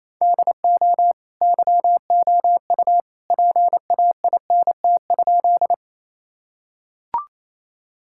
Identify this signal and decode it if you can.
Morse code, heard as a keyed tone, about 28 wpm, 700 Hz. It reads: DO YOU PAINT?
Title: Sentences from Top 1000 Words - Review Part 6 - 28wpm